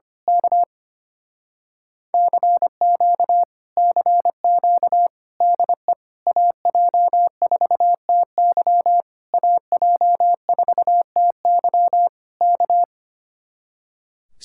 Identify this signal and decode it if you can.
Morse code, heard as a keyed tone, about 24 wpm, 700 Hz. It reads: K CQ CQ DE AJ4TY AJ4TY K
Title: Speed-Racing Common QSO Phrases - 25wpm